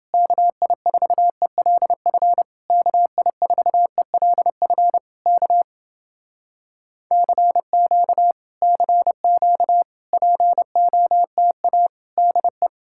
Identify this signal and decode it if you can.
Morse code, heard as a keyed tone, about 29 wpm, 700 Hz. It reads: KI4ELF KI4ELF K CQ CQ POTA DE